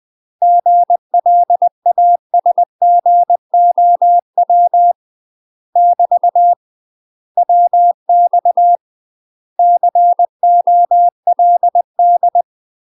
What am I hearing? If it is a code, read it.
Morse code, heard as a keyed tone, about 20 wpm, 700 Hz. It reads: GLASGOW <BT> WX COLD
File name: UK MASTER QSOs TRAINING / 20WPM / UK MASTER QSOs 0056